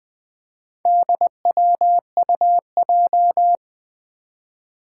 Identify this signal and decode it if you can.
Morse code, heard as a keyed tone, about 21 wpm, 700 Hz. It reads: DWUJ
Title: Four Challenging Letter-Number Combinations 20wpm